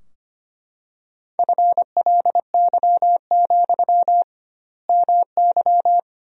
Morse code, heard as a keyed tone, about 25 wpm, 700 Hz. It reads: FLY, MY
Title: 2x Word-Spacing - Sentences from Top 500 Words - 25wpm